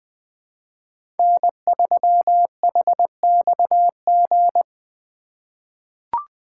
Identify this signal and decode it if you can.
Morse code, heard as a keyed tone, about 20 wpm, 700 Hz. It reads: N3HXG
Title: USA Call Signs - 5 Characters - 4x Character Spacing - 20wpm